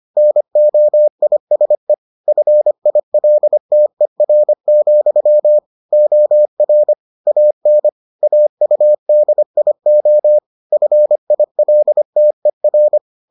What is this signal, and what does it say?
Morse code, heard as a keyed tone, about 25 wpm, 600 Hz. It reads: NOISE FILTER, OR AN AUDIO FILTER